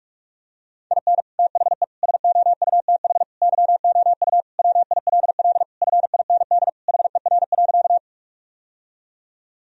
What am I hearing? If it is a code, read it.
Morse code, heard as a keyed tone, about 42 wpm, 700 Hz. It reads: IN THE SOUTH YOU WILL FIND HER.
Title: Sentences from Top 1000 Words - Review Part 2 - 45wpm